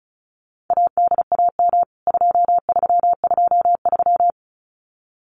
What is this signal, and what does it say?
Morse code, heard as a keyed tone, about 34 wpm, 700 Hz. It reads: ADAM 2323
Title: CWT Exchange 35wpm